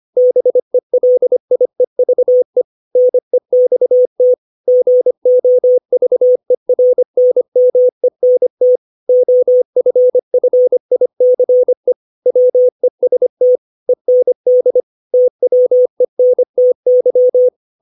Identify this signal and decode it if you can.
Morse code, heard as a keyed tone, about 25 wpm, 500 Hz. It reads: BELIEVE NEXT GOVERNMENT OFFICE WEST END TWENTY
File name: 632 words / 632 Words 25WPM